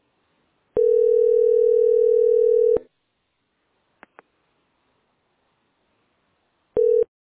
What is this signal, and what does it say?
Ringback tone (North America), 440 + 480 Hz.